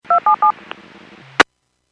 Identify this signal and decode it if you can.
DTMF, heard as telephone touch tones, tones of about 110 ms. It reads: 277